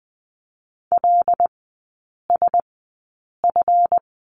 Morse code, heard as a keyed tone, about 20 wpm, spaced as for 8 wpm, 700 Hz. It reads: LSF
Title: 138 Lesson 23e - Letter f - Three Characters 20wpm